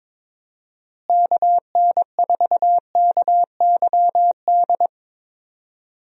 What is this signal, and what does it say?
Morse code, heard as a keyed tone, about 22 wpm, 700 Hz. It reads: KN4KYD
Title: USA Call Signs - 6 Characters - Repeated 2x - 22wpm